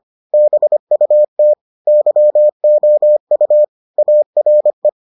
Morse code, heard as a keyed tone, about 25 wpm, 600 Hz. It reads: BUT YOU ARE